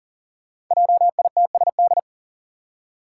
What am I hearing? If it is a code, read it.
Morse code, heard as a keyed tone, about 40 wpm, 690 Hz. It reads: JITSD
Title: Five Challenging Letter-Number Combinations 40wpm